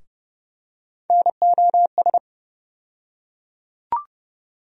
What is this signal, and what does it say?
Morse code, heard as a keyed tone, about 29 wpm, 700 Hz. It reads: NOS
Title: USA Call Sign Postfixes - 3 Letters 30wpm